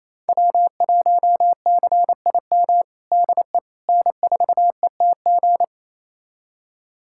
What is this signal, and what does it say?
Morse code, heard as a keyed tone, about 27 wpm, 700 Hz. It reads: W1CIM DE N4ETG
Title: USA Callsign DE Callsign - 5 Characters - 28wpm